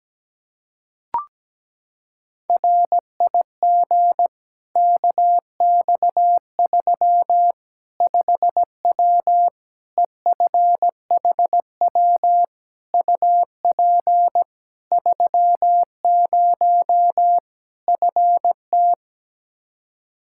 Morse code, heard as a keyed tone, about 17 wpm, 700 Hz. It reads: RIG KX3 5W EFHW UP 30 FT
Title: Practice QSO - Exchange 2 - Info - 17wpm